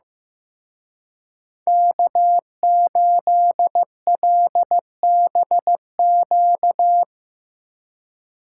Morse code, heard as a keyed tone, about 17 wpm, 700 Hz. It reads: K8LBQ
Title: Speed Racing - USA 1x3 Call Signs 15wpm